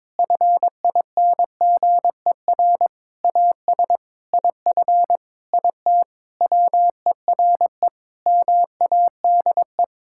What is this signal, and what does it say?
Morse code, heard as a keyed tone, about 22 wpm, 700 Hz. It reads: FINGER AS IF IT WERE MADE
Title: Sentences from Top 1600 Words 22wpm